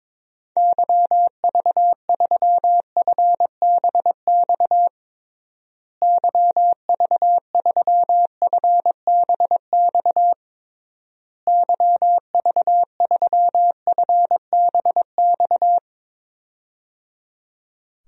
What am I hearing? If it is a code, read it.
Morse code, heard as a keyed tone, about 22 wpm, 700 Hz. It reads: YV3FBX YV3FBX YV3FBX
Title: DXCC-100 Call Signs - 6 Characters - Repeated 3x - 22wpm